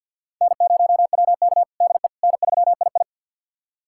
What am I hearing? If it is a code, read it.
Morse code, heard as a keyed tone, about 52 wpm, 700 Hz. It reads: N0WK DE N3II
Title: USA Callsign DE Callsign - 4 Characters - 50wpm